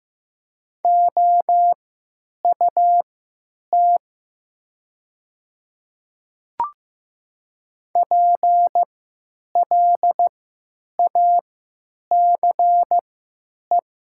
Morse code, heard as a keyed tone, about 15 wpm, spaced as for 9 wpm, 700 Hz. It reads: OUT PLACE
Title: 3x Character Spacing - Top 200 Words - 15wpm